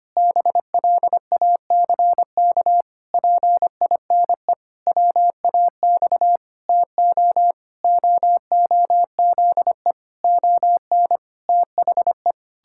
Morse code, heard as a keyed tone, about 25 wpm, 700 Hz. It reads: BLACK PINE WAX TO OOZE ON THE
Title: Pangram Challenge v2 - 25wpm